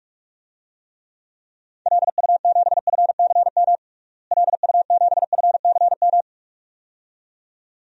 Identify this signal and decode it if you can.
Morse code, heard as a keyed tone, about 46 wpm, 700 Hz. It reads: LU7FCM LU7FCM
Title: DXCC-100 Call Signs - 6 Characters - Repeated 2x - 45wpm